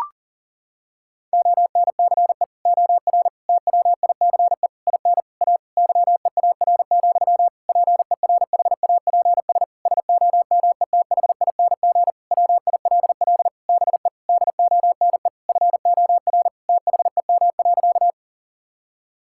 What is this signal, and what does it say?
Morse code, heard as a keyed tone, about 40 wpm, 700 Hz. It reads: ONCE OR TWICE IN A YEAR, PERHAPS SOMETHING WILL BE DONE FOR THEM.
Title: Sentences from Top 1600 Words 40wpm